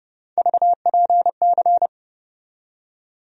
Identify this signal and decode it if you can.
Morse code, heard as a keyed tone, about 30 wpm, 700 Hz. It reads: VPC
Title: USA Call Sign Postfixes - 3 Letters 30wpm